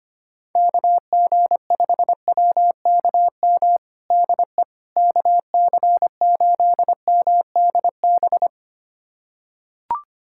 Morse code, heard as a keyed tone, about 25 wpm, 700 Hz. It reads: KG5WKM DE KC8MDB
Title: USA Callsign DE Callsign 25wpm